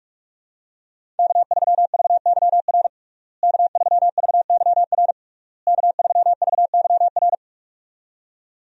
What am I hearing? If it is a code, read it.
Morse code, heard as a keyed tone, about 43 wpm, 700 Hz. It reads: K3VYR K3VYR K3VYR
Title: USA Call Signs - 5 Characters - Repeated 3x - 45wpm